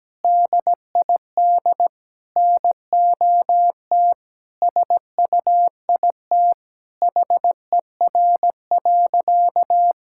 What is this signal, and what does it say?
Morse code, heard as a keyed tone, about 17 wpm, 700 Hz. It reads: DID NOT SUIT HER.
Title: Sentences from Top 1600 Words 17wpm